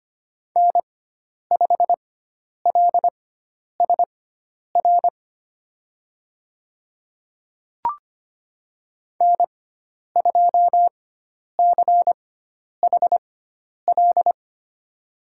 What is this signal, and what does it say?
Morse code, heard as a keyed tone, about 25 wpm, spaced as for 10 wpm, 700 Hz. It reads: N5LSR N2CHL
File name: Lesson 16g - Letter c - Call Signs 25wpm